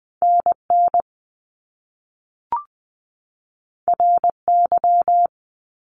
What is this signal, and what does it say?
Morse code, heard as a keyed tone, about 20 wpm, 700 Hz. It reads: NN RY
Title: Top Two Letter Combinations in Words 20wpm